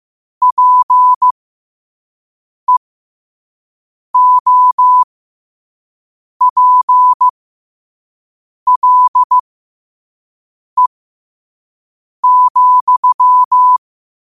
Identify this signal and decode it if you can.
Morse code, heard as a keyed tone, about 15 wpm, spaced as for 5 wpm, 1000 Hz. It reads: PEOPLE,
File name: qotdfiles / 5wpm